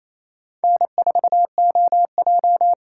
Morse code, heard as a keyed tone, about 28 wpm, 700 Hz. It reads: N4OJ
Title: USA Call Signs - 2x Character Spacing - 28wpm